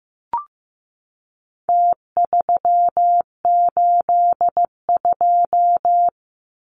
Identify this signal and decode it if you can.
Morse code, heard as a keyed tone, about 15 wpm, 700 Hz. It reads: T382
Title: Numbers - Contest Style - 4-Digit 15wpm